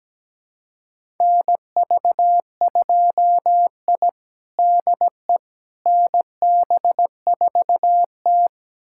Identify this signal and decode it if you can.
Morse code, heard as a keyed tone, about 17 wpm, 700 Hz. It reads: NV2I DE NB4T
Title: USA Callsign DE Callsign - 4 Characters - 17wpm